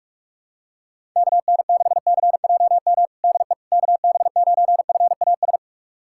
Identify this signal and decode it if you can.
Morse code, heard as a keyed tone, about 43 wpm, 690 Hz. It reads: KN6CJM DE KB9FAS
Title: USA Callsign DE Callsign 45wpm